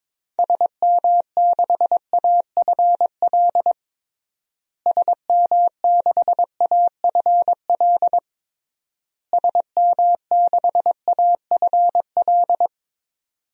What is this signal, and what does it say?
Morse code, heard as a keyed tone, about 22 wpm, 700 Hz. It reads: SM6AFL SM6AFL SM6AFL
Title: DXCC-100 Call Signs - 6 Characters - Repeated 3x - 22wpm